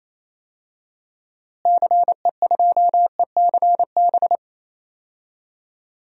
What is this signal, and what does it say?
Morse code, heard as a keyed tone, about 27 wpm, 700 Hz. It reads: CE2ECB
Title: DXCC-100 Call Signs - 6 Characters - Repeated 3x - 28wpm